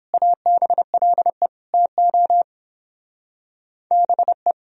Morse code, heard as a keyed tone, about 29 wpm, 700 Hz. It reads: ABLE TO BE